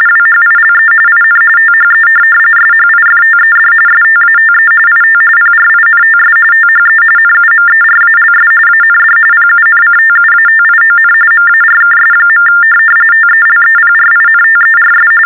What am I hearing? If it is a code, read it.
TTY/TDD text telephone tones, about 45.45 baud. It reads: WHILE IN THE SHIFTED STATE IT PRINTS THE NUMERALS AND PUNCTUATION MARKS. TELEPRINTERS
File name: teletype-RTTY-45